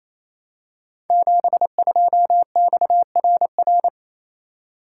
Morse code, heard as a keyed tone, about 29 wpm, 700 Hz. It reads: 72XRR
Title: Code Groups - 5 Letter:Numbers - 28wpm